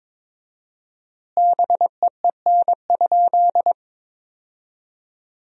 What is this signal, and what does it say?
Morse code, heard as a keyed tone, about 23 wpm, 700 Hz. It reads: BEEN?
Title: 4x Word-Spacing - Sentences from Top 400 Words - 22wpm